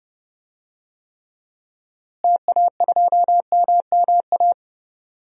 Morse code, heard as a keyed tone, about 32 wpm, 700 Hz. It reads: TA2MMA
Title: DXCC-100 Call Signs - 6 Characters - Repeated 2x - 30wpm